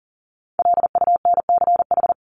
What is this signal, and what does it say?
Morse code, heard as a keyed tone, about 40 wpm, 700 Hz. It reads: LUNCH
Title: Top 1751-2000 Words 40wpm